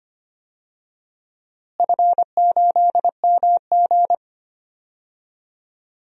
Morse code, heard as a keyed tone, about 25 wpm, 700 Hz. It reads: F8MG
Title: DXCC-100 Call Signs - 4 Characters - Repeated 3x - 25wpm